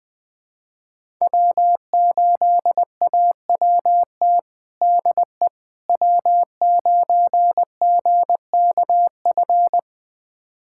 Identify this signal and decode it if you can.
Morse code, heard as a keyed tone, about 20 wpm, 700 Hz. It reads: W8AWT DE W9GKF
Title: USA Callsign DE Callsign - 5 Characters - 20wpm